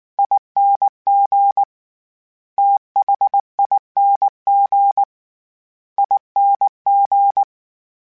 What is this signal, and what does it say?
Morse code, heard as a keyed tone, about 19 wpm, 800 Hz. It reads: ING THING ING
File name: Special INGs / Special INGs-mp3s / VerySlowToFaster Ing @18